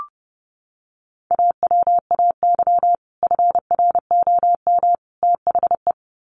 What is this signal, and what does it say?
Morse code, heard as a keyed tone, about 30 wpm, 700 Hz. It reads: AWAY FROM THE
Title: Sets of 3 Words 30wpm